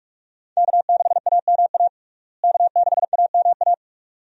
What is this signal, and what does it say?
Morse code, heard as a keyed tone, about 41 wpm, 700 Hz. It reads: K6AMA K6AMA
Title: USA Call Signs - 5 Characters - Repeated 2x - 45wpm